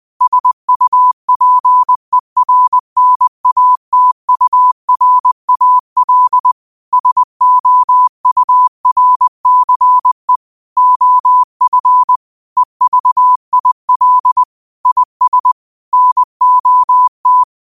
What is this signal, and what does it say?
Morse code, heard as a keyed tone, about 20 wpm, 1000 Hz. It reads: SUPERNATURAL SOURCE OF EVIL IS NOT